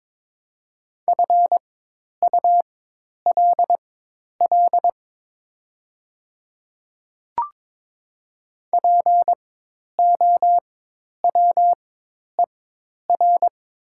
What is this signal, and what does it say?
Morse code, heard as a keyed tone, about 22 wpm, spaced as for 10 wpm, 700 Hz. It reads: FULL POWER
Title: 4x Character Spacing - Top 500 Words - 22wpm